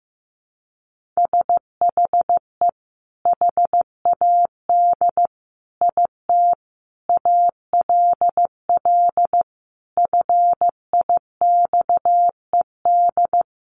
Morse code, heard as a keyed tone, about 15 wpm, 700 Hz. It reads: SHE HAD IT ALL FIXED
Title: Sentences from Top 1100 Words 15wpm